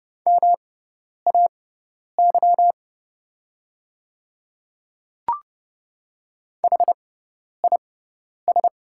Morse code, heard as a keyed tone, about 30 wpm, spaced as for 10 wpm, 700 Hz. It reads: MAY HIS
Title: 6x Character Spacing - Top 100 Words - 30wpm